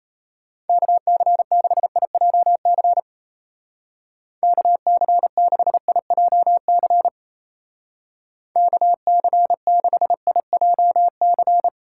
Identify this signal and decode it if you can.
Morse code, heard as a keyed tone, about 36 wpm, 700 Hz. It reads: KC6IJC KC6IJC KC6IJC